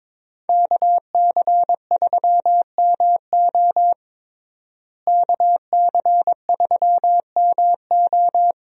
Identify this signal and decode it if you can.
Morse code, heard as a keyed tone, about 22 wpm, 700 Hz. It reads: KC3MO KC3MO
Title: USA Call Signs - 5 Characters - Repeated 3x - 22wpm